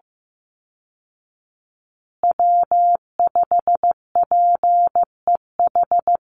Morse code, heard as a keyed tone, about 15 wpm, 700 Hz. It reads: W5PEH
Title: Call Signs Repeated 3x - 15wpm